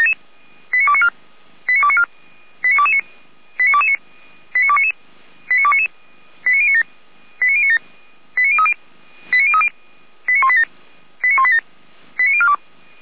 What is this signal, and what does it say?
Five-tone selective call (ZVEI), heard as quick runs of five tones, tones of about 70 ms. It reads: 89294 89294 89229 89229 89299 89299 89097 89097 89030 89030 89187 89187 89042